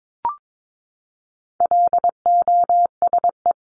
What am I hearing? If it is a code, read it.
Morse code, heard as a keyed tone, about 22 wpm, 700 Hz. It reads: LOSE
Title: Top 1001-1250 Words 22wpm